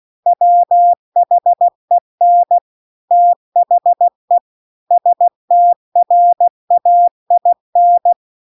Morse code, heard as a keyed tone, about 16 wpm, 700 Hz. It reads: WHEN THE STRAIN
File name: CW-The-Big-Twoheaded-River-2-16WPM